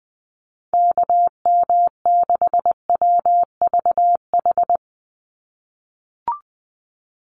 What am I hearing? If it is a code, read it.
Morse code, heard as a keyed tone, about 19 wpm, 700 Hz. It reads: KM6WVH
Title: USA Call Signs 20wpm